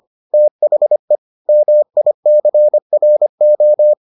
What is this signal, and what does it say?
Morse code, heard as a keyed tone, about 25 wpm, 600 Hz. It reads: THE MICRO